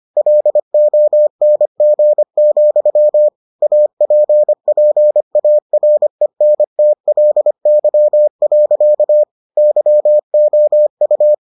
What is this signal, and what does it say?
Morse code, heard as a keyed tone, about 25 wpm, 600 Hz. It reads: LONG, APPARENTLY. YOU